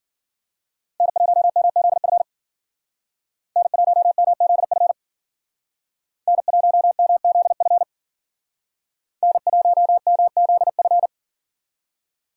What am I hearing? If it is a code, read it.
Morse code, heard as a keyed tone, about 50 wpm, 700 Hz. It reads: N1MZF N1MZF N1MZF N1MZF
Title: Speed Racing - USA 1x3 Call Signs 40wpm